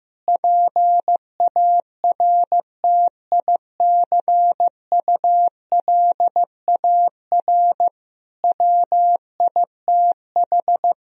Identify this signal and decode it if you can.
Morse code, heard as a keyed tone, about 16 wpm, 700 Hz. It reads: PARTICULAR WITH
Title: Sentences from Top 1000 Words - Review Part 4 - 15wpm